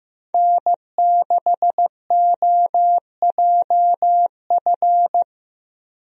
Morse code, heard as a keyed tone, about 15 wpm, 700 Hz. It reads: N6OJF
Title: USA Call Signs - 5 Characters - Repeated 2x - 15wpm